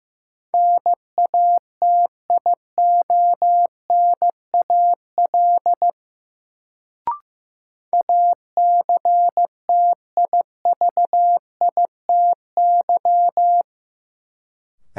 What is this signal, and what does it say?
Morse code, heard as a keyed tone, about 15 wpm, 700 Hz. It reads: NATIONAL ACTIVITY
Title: Sentences from Top 1000 Words - Review Part 4 - New Word Drill - 15wpm